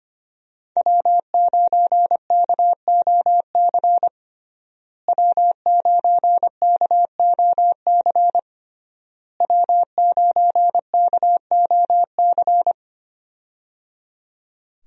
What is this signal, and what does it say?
Morse code, heard as a keyed tone, about 25 wpm, 700 Hz. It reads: W9KOC W9KOC W9KOC
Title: USA Call Signs - 5 Characters - Repeated 3x - 25wpm